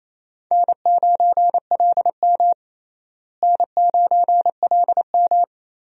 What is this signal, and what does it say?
Morse code, heard as a keyed tone, about 28 wpm, 700 Hz. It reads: N9LM N9LM
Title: USA Call Signs - 4 Characters - Repeated 3x - 28wpm